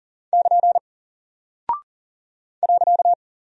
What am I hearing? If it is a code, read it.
Morse code, heard as a keyed tone, about 40 wpm, 700 Hz. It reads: <KN> .
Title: Single Character (Letter:Number:Punctuation:Prosign) Alt 40wpm